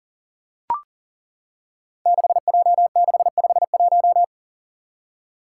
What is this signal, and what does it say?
Morse code, heard as a keyed tone, about 37 wpm, 700 Hz. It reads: 6J651
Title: Five Challenging Letter-Number Combinations 40wpm